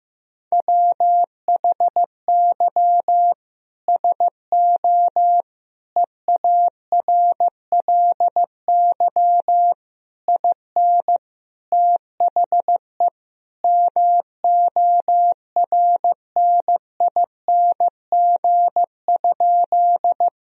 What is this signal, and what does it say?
Morse code, heard as a keyed tone, about 15 wpm, 700 Hz. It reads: WHY SO EARLY IN THE MORNING?
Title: Sentences from Top 1000 Words - Review Part 2 - 15wpm